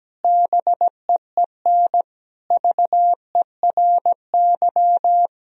Morse code, heard as a keyed tone, about 17 wpm, 700 Hz. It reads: BEEN VERY
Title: Sentences from Top 1000 Words - Review Part 3 - 17wpm